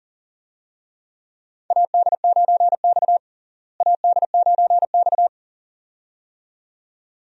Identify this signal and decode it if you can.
Morse code, heard as a keyed tone, about 40 wpm, 700 Hz. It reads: AD9X AD9X
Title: USA Call Signs - 4 Characters - Repeated 2x - 40wpm